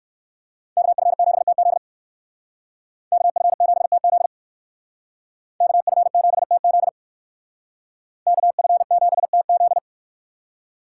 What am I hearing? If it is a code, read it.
Morse code, heard as a keyed tone, about 58 wpm, 700 Hz. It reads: KF7TZ KF7TZ KF7TZ KF7TZ